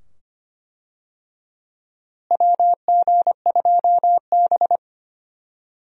Morse code, heard as a keyed tone, about 25 wpm, 700 Hz. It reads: WG2B
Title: USA Call Signs - 4 Characters - Repeated 2x - 25wpm